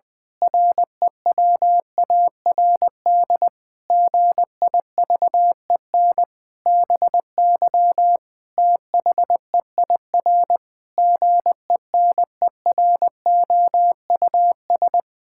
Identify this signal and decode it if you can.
Morse code, heard as a keyed tone, about 20 wpm, 700 Hz. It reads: REWARD GIVEN BY THEIR GENEROUS
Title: Pangram Challenge v2 - 20wpm